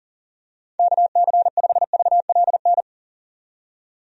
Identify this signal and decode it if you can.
Morse code, heard as a keyed tone, about 37 wpm, 700 Hz. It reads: KC5VLN
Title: USA Call Signs - 6 Characters - 40wpm